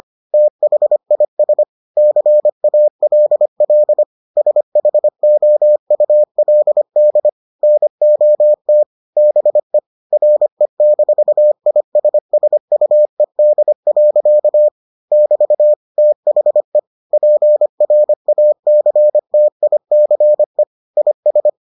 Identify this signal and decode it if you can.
Morse code, heard as a keyed tone, about 25 wpm, 600 Hz. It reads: THIS CALL SHOULD NOT BE RE-ISSUED. <BT> THE PRACTICE IS